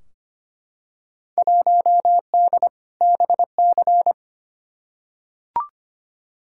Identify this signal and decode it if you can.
Morse code, heard as a keyed tone, about 25 wpm, 700 Hz. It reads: 1D BC
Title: Field Day Exchanges - 25wpm